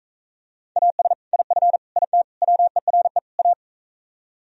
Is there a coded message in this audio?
Yes: Morse code from 0.8 to 3.5 s, heard as a keyed tone, about 45 wpm, 700 Hz. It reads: AS IF IT WERE A